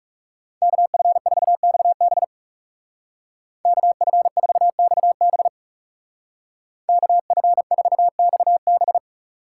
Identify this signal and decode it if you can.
Morse code, heard as a keyed tone, about 36 wpm, 700 Hz. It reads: KF4XB KF4XB KF4XB
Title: Speed Racing - USA 2x2 Call Signs 30wpm